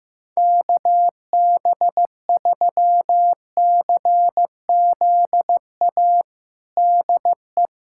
Morse code, heard as a keyed tone, about 16 wpm, 700 Hz. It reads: KB3CZA DE
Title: USA Callsign DE Callsign - 6 Characters - 15wpm